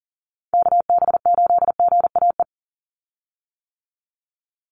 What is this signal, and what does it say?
Morse code, heard as a keyed tone, about 41 wpm, 700 Hz. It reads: KB8GAE
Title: Call Signs - 40wpm